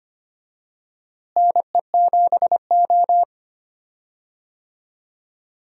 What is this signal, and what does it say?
Morse code, heard as a keyed tone, about 25 wpm, 700 Hz. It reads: NE7O
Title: USA Call Signs - 4 Characters - Repeated 2x - 25wpm